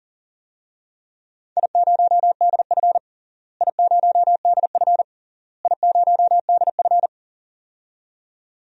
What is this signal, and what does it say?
Morse code, heard as a keyed tone, about 40 wpm, 700 Hz. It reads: I0DF I0DF I0DF
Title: DXCC-100 Call Signs - 4 Characters - Repeated 3x - 40wpm